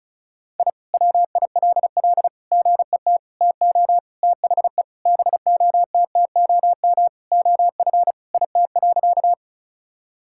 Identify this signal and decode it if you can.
Morse code, heard as a keyed tone, about 36 wpm, 700 Hz. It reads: I WILL GET TO THE BOTTOM OF IT.
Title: Sentences from Top 1000 Words - Review Part 6 - 35wpm